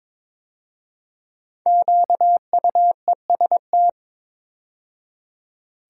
Morse code, heard as a keyed tone, about 22 wpm, 700 Hz. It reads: QUEST
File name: 5 Letter Words - First 3 Letters Given - 22wpm